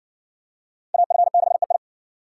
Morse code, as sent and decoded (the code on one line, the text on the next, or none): .- ..-. -.... ..
AF6EI